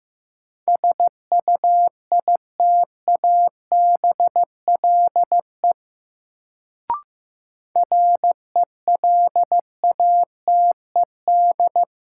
Morse code, heard as a keyed tone, about 15 wpm, 700 Hz. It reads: SUITABLE RELATED
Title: Sentences from Top 1400 Words - New Word Drill Alt - 15wpm